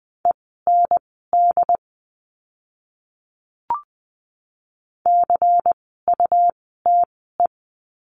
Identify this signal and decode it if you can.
Morse code, heard as a keyed tone, about 20 wpm, spaced as for 15 wpm, 700 Hz. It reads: END CUTE
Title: 101 Lesson 17f - Letter u - Words 20wpm